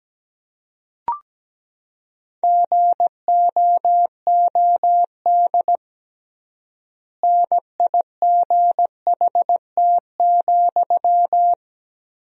Morse code, heard as a keyed tone, about 17 wpm, 700 Hz. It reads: GOOD NIGHT,
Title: 3x Word-Spacing - Sentences from Top 300 Words - 17wpm